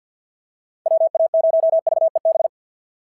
Morse code, heard as a keyed tone, about 49 wpm, 690 Hz. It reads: WA0VEB
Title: SOTA QSO Practice 50wpm